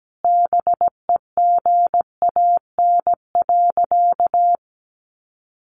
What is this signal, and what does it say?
Morse code, heard as a keyed tone, about 17 wpm, 700 Hz. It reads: BEGAN.
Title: Sentences from Top 1100 Words 17wpm